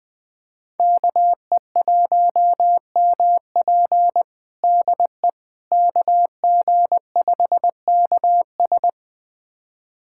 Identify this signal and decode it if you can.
Morse code, heard as a keyed tone, about 20 wpm, 700 Hz. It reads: KE1MP DE KG5KS
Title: USA Callsign DE Callsign - 5 Characters - 20wpm